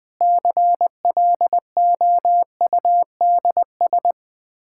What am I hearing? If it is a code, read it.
Morse code, heard as a keyed tone, about 20 wpm, 700 Hz. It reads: CLOUDS
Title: Practice QSO - Exchange 2 - Info - 20wpm